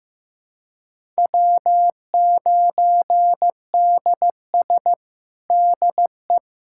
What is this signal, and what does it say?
Morse code, heard as a keyed tone, about 16 wpm, 700 Hz. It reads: W9DS DE
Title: USA Callsign DE Callsign 15wpm